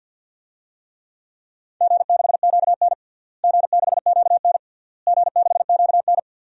Morse code, heard as a keyed tone, about 49 wpm, 700 Hz. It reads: G6QN G6QN G6QN
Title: DXCC-100 Call Signs - 4 Characters - Repeated 3x - 50wpm